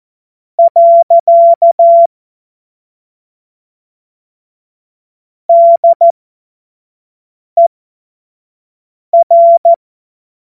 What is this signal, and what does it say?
Morse code, heard as a keyed tone, about 14 wpm, spaced as for 5 wpm, 680 Hz. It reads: . DER
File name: Morsekurs / Witze / Witz 21. Juni 2022 25 F70